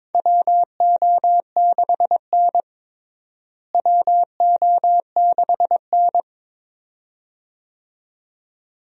Morse code, heard as a keyed tone, about 23 wpm, 700 Hz. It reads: WO6N WO6N
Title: USA Call Signs - 4 Characters - Repeated 2x - 22wpm